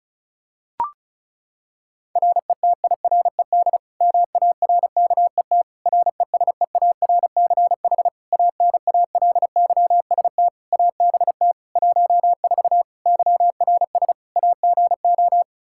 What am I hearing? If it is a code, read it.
Morse code, heard as a keyed tone, about 36 wpm, 700 Hz. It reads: RETIRED MARKET RESEARCH ANALYST ABT 14 YRS AGO
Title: Practice QSO - Exchange 3 - More Info - 35wpm